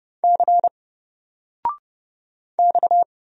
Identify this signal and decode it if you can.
Morse code, heard as a keyed tone, about 30 wpm, 700 Hz. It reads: C X
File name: Single-Letter-Number-30wpm